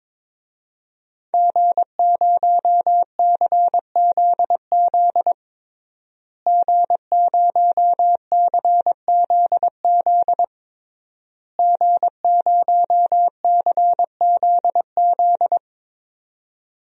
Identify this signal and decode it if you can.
Morse code, heard as a keyed tone, about 22 wpm, 700 Hz. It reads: G0CZZ G0CZZ G0CZZ
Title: DXCC-100 Call Signs - 5 Characters - Repeated 3x - 22wpm